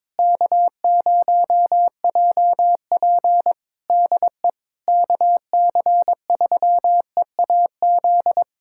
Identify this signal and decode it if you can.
Morse code, heard as a keyed tone, about 22 wpm, 700 Hz. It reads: K0JP DE KC3EAZ
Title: Practice QSO - Exchange 4 - Ending - 22wpm